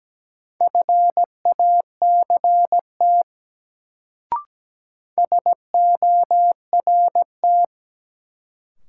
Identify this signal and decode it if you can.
Morse code, heard as a keyed tone, about 17 wpm, 700 Hz. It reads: FACT SORT
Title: Sets of 2 Words - New Word Drill - 17wpm